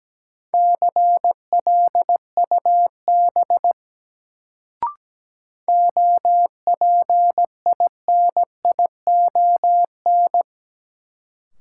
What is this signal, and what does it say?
Morse code, heard as a keyed tone, about 17 wpm, 700 Hz. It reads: CLUB OPINION
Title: Sentences from Top 1200 Words - New Word Drill - 17wpm